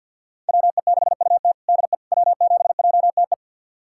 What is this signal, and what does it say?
Morse code, heard as a keyed tone, about 51 wpm, 700 Hz. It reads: WE6UT DE W7JTE
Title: USA Callsign DE Callsign - 5 Characters - 50wpm